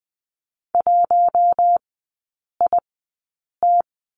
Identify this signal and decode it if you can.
Morse code, heard as a keyed tone, about 20 wpm, spaced as for 8 wpm, 700 Hz. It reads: 1IT
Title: 041 Lesson 8e - Number 1 - Three Characters 20wpm